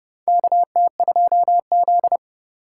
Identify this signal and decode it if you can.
Morse code, heard as a keyed tone, about 30 wpm, 700 Hz. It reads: KT2Z
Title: USA Call Signs - 3x Character Spacing - 30wpm